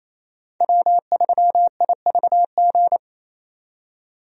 Morse code, heard as a keyed tone, about 28 wpm, 700 Hz. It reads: W3IVG
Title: USA Call Signs - 5 Characters - 28wpm